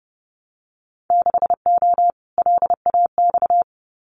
Morse code, heard as a keyed tone, about 30 wpm, 700 Hz. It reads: 6O LAX
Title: Winter Field Day Exchanges Encore 30wpm